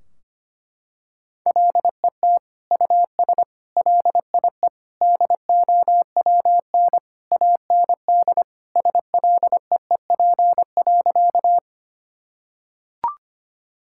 Morse code, heard as a keyed tone, about 25 wpm, 700 Hz. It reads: LET US LIE DOWN AND SLEEP.
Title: Sentences from Top 1600 Words 25wpm